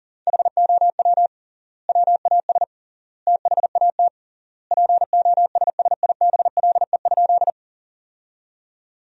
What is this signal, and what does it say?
Morse code, heard as a keyed tone, about 40 wpm, 700 Hz. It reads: HOW WAS THAT POSSIBLE?
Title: 3x Word-Spacing - Sentences from Top 500 Words - 40wpm